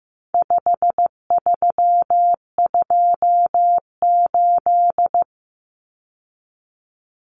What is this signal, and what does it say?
Morse code, heard as a keyed tone, about 15 wpm, 700 Hz. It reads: 5328
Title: Numbers - 4-Digit 15wpm